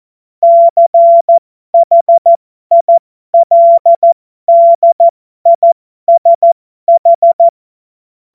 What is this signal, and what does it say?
Morse code, heard as a keyed tone, about 14 wpm, spaced as for 12 wpm, 680 Hz. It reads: CHILDISH
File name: Morsekurs / Witze